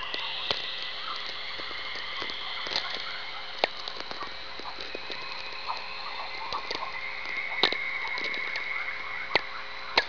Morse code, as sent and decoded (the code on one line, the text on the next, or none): none